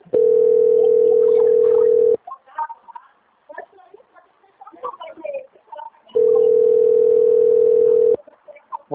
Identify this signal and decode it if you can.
Ringback tone (North America), 440 + 480 Hz.
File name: out- - - - - .43395